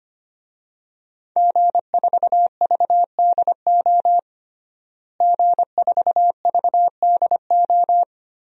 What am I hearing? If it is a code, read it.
Morse code, heard as a keyed tone, about 25 wpm, 700 Hz. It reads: G4VDO G4VDO
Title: DXCC-100 Call Signs - 5 Characters - Repeated 3x - 25wpm